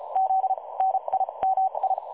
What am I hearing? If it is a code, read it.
Morse code, heard as a keyed tone, about 32 wpm, spaced as for 27 wpm, 760 Hz. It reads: Z NIMI
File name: dlazaawansowanych / nagraniazpasma / websdr recording 3565.0kHz